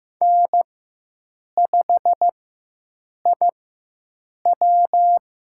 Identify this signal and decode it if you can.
Morse code, heard as a keyed tone, about 15 wpm, spaced as for 7 wpm, 700 Hz. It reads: N5IW
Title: USA Call Signs - 4 Characters - 4x Character Spacing - 15wpm